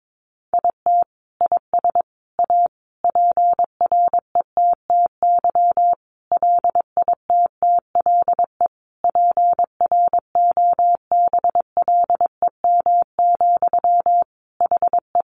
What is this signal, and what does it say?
Morse code, heard as a keyed tone, about 22 wpm, 700 Hz. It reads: IT IS A PRETTY LITTLE PROBLEM, HE
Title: Sentences from Top 1000 Words 22wpm